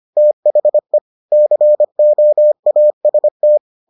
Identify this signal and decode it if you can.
Morse code, heard as a keyed tone, about 25 wpm, 600 Hz. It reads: THE COAST